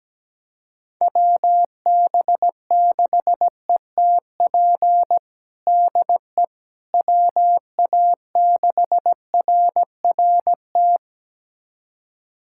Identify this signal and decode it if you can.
Morse code, heard as a keyed tone, about 17 wpm, 700 Hz. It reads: WB6ETP DE WA6RRT
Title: USA Callsign DE Callsign - 6 Characters - 17wpm